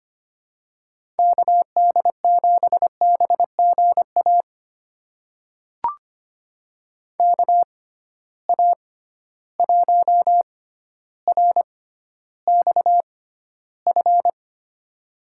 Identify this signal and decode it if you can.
Morse code, heard as a keyed tone, about 25 wpm, spaced as for 13 wpm, 700 Hz. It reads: KD7BGA KA1RXF
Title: USA Call Signs - 6 Characters - 6x Character Spacing - 25wpm